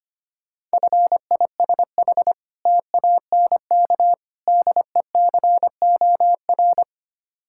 Morse code, heard as a keyed tone, about 26 wpm, 700 Hz. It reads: FISH TANK DECOR
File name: practice / 25wpm / Difficult Words - Speed-Racing with Hints - 25wpm